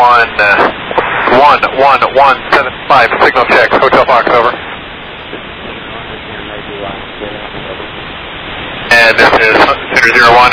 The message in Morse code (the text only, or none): none